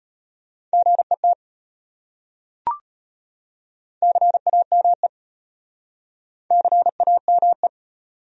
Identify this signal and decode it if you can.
Morse code, heard as a keyed tone, about 33 wpm, 700 Hz. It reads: GET CAME CAME